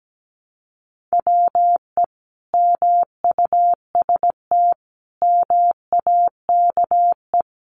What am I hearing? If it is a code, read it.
Morse code, heard as a keyed tone, about 17 wpm, 700 Hz. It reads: WE MUST MAKE
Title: Sentences from Top 400 Words 17wpm